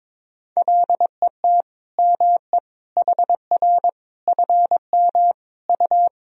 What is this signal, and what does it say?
Morse code, heard as a keyed tone, about 21 wpm, 700 Hz. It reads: LET ME HR FM U
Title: Sentences from Top 400 Words and Abbreviations 22wpm